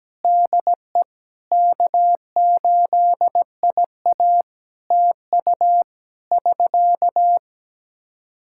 Morse code, heard as a keyed tone, about 17 wpm, 700 Hz. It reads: DE K8IA TU <SK>
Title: K1USN SST QSO Practice 17wpm